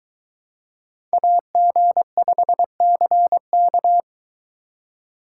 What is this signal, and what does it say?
Morse code, heard as a keyed tone, about 22 wpm, 700 Hz. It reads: AG5CK